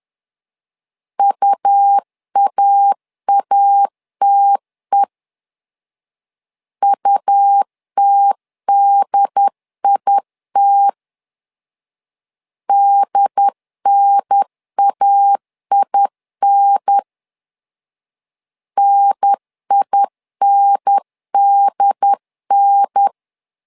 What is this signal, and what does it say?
Morse code, heard as a keyed tone, about 10 wpm, 800 Hz. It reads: UAATE UTDIT DNAIN NINDN
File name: LEZIONI / 2 UHDN / 2 UHDN 10wpm 1904